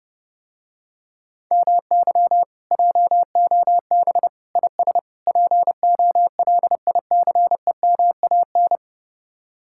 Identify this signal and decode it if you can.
Morse code, heard as a keyed tone, about 30 wpm, 700 Hz. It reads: MY JOB IS POLICEMAN